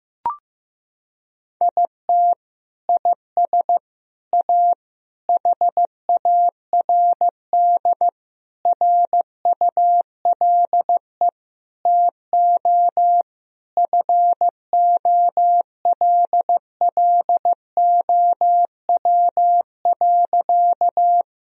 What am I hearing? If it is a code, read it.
Morse code, heard as a keyed tone, about 15 wpm, 700 Hz. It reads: IT IS A HARD RULE TO FOLLOW.
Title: Sentences from Top 1000 Words - Review Part 2 - 15wpm